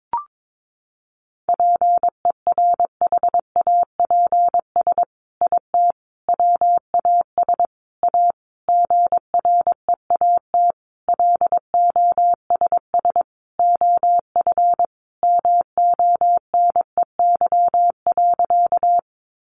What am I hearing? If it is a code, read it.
Morse code, heard as a keyed tone, about 22 wpm, 700 Hz. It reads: PERHAPS IT WAS A GREAT LOSS OF MONEY.
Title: Sentences from Top 1400 Words 22wpm